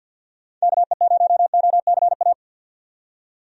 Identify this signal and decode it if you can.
Morse code, heard as a keyed tone, about 53 wpm, 700 Hz. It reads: KE0OCA
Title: USA Call Signs - 6 Characters - 50wpm